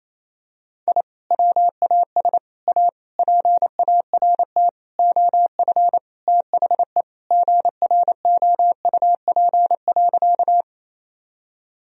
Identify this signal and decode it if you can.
Morse code, heard as a keyed tone, about 28 wpm, 700 Hz. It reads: I WAS A PART OF THE GROUP.
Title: Sentences from Top 1000 Words - Review Part 1 - 28wpm